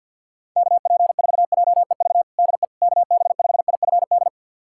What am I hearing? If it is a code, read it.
Morse code, heard as a keyed tone, about 52 wpm, 700 Hz. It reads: KP4JEV DE KB5IFD
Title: USA Callsign DE Callsign 50wpm